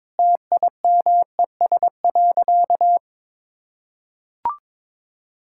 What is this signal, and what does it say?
Morse code, heard as a keyed tone, about 22 wpm, 700 Hz. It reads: TIMES.
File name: practice / 22wpm / Sentences with 8-Word Phrases - 22wpm